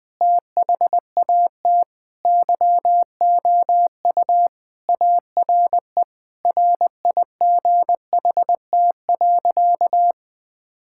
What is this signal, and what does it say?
Morse code, heard as a keyed tone, about 20 wpm, 700 Hz. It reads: THAT YOU ARE RIGHT.
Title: Sentences with 5-Word Phrases - 20wpm